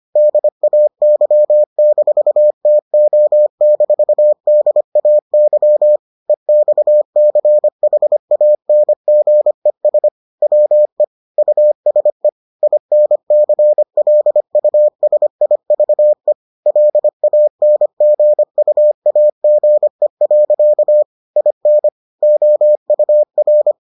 Morse code, heard as a keyed tone, about 25 wpm, 600 Hz. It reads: DAY-TO-DAY EXCHANGES WE USE INCLUSIVE LANGUAGE. IN OUR